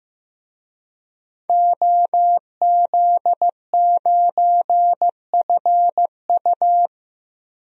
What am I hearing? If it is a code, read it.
Morse code, heard as a keyed tone, about 15 wpm, 700 Hz. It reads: OZ9FU
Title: DXCC-100 Call Signs - 5 Characters - 15wpm